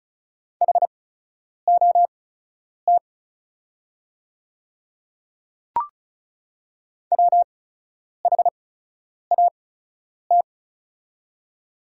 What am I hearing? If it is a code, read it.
Morse code, heard as a keyed tone, about 34 wpm, spaced as for 10 wpm, 690 Hz. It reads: HOT WHAT
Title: 8x Character Spacing - Top 100 Words - 35wpm